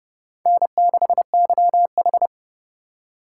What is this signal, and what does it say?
Morse code, heard as a keyed tone, about 38 wpm, 700 Hz. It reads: N6YH
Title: Speed Racing - USA 1x2 Call Signs 30wpm